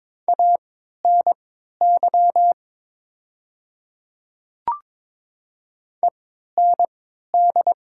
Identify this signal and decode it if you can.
Morse code, heard as a keyed tone, about 22 wpm, spaced as for 13 wpm, 700 Hz. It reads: ANY END
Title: 3x Character Spacing - Top 200 Words - 22wpm